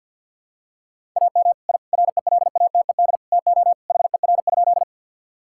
Morse code, heard as a keyed tone, about 52 wpm, 700 Hz. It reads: AM I RELATED TO HER?
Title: Sentences from Top 1000 Words - Review Part 5 - 50wpm